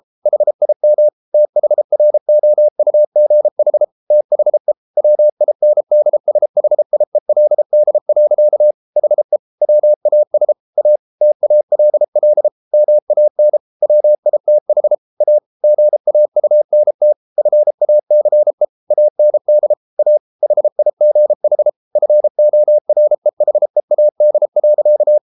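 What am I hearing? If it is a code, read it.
Morse code, heard as a keyed tone, about 33 wpm, 600 Hz. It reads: HIM THROUGH THE WINDSHIELD. HE WAS A TALL MAN WITH A GAUNT FACE AND A HIGH FOREHEAD.
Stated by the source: Morse code practice at 32 wpm from CW med Gnister, reading Ken Follett's novel Never in English.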